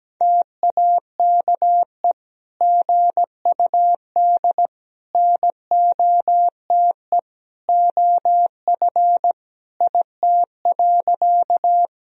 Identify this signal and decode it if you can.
Morse code, heard as a keyed tone, about 17 wpm, 700 Hz. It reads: TAKE GUD NOTE OF IT.
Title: Sentences from Top 500 Words and Abbreviations 17wpm